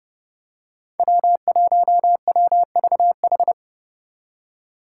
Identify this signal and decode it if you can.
Morse code, heard as a keyed tone, about 29 wpm, 700 Hz. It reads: W1WVH
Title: Five Challenging Letter-Number Combinations 30wpm